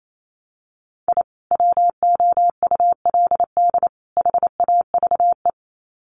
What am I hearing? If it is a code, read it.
Morse code, heard as a keyed tone, about 28 wpm, 700 Hz. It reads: I WOULD HAVE
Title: Sets of 3 Words 28wpm